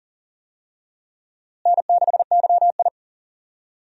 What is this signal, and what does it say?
Morse code, heard as a keyed tone, about 40 wpm, 700 Hz. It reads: N6YI
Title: USA Call Signs - 4 Characters - Repeated 2x - 40wpm